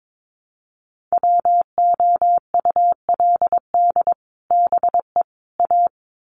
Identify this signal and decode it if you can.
Morse code, heard as a keyed tone, about 22 wpm, 700 Hz. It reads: WOULD BE A
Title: Sets of 3 Words 22wpm